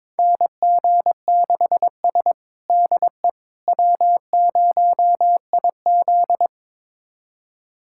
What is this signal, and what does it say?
Morse code, heard as a keyed tone, about 22 wpm, 700 Hz. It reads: NG6S DE W0IZ
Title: USA Callsign DE Callsign - 4 Characters - 22wpm